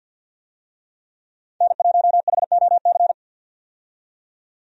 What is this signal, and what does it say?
Morse code, heard as a keyed tone, about 52 wpm, 700 Hz. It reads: N1HOC
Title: USA Call Signs - 5 Characters - 50wpm